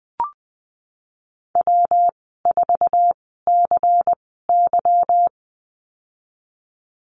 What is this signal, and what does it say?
Morse code, heard as a keyed tone, about 19 wpm, spaced as for 14 wpm, 700 Hz. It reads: W4CY
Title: 154 Lesson 25g - Letter p - Call Signs 20wpm